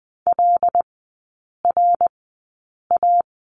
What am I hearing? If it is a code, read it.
Morse code, heard as a keyed tone, about 20 wpm, spaced as for 8 wpm, 700 Hz. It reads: LRA
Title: 166 Lesson 27e - Number 7 - Three Characters 20wpm